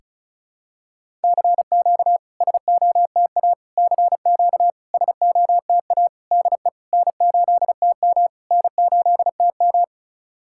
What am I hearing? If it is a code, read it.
Morse code, heard as a keyed tone, about 36 wpm, 690 Hz. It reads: CQ SOTA CQ SOTA DE N8TM N8TM
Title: SOTA QSO Practice 35wpm